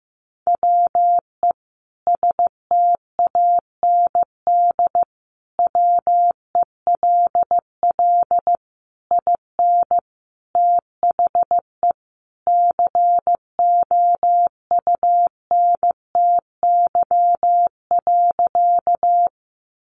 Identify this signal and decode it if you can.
Morse code, heard as a keyed tone, about 15 wpm, 700 Hz. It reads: WE STAND WELL IN THE COUNTY.
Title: Sentences from Top 1200 Words 15wpm